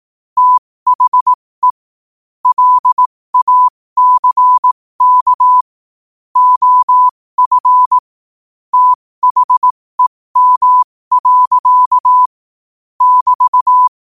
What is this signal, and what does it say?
Morse code, heard as a keyed tone, about 18 wpm, spaced as for 15 wpm, 1000 Hz. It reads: THE LACK OF THEM. <BT>
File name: qotdfiles / 15wpm